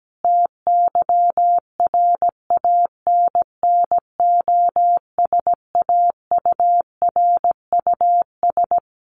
Morse code, heard as a keyed tone, about 17 wpm, 700 Hz. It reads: TYRANNOSAURUS
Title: A Sound Of Thunder 17wpm